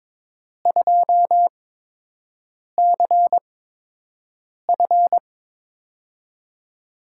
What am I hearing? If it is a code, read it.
Morse code, heard as a keyed tone, about 21 wpm, spaced as for 6 wpm, 700 Hz. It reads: 2CF
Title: USA Call Signs - 4 Characters - 8x Character Spacing - 22wpm